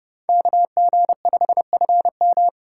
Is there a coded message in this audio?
Yes: Morse code from 0.3 to 2.5 s, heard as a keyed tone, about 29 wpm, 700 Hz. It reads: KG5FM